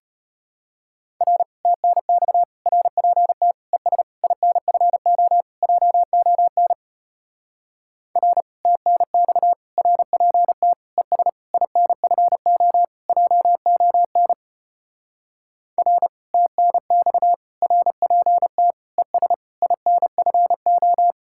Morse code, heard as a keyed tone, about 33 wpm, 700 Hz. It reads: R TNX RPT ES INFO JON R TNX RPT ES INFO JON R TNX RPT ES INFO